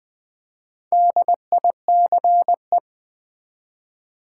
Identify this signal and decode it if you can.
Morse code, heard as a keyed tone, about 20 wpm, 700 Hz. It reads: DICE
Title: Easy Words with Hints - 20wpm